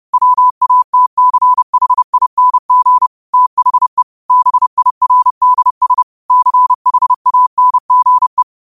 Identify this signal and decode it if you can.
Morse code, heard as a keyed tone, about 30 wpm, 1000 Hz. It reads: WATCHING THE BIRDS CHANGE
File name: 30wpm